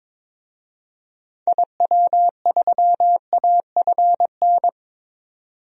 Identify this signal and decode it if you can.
Morse code, heard as a keyed tone, about 22 wpm, 700 Hz. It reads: IW3AFN
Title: DXCC-100 Call Signs - 6 Characters - 22wpm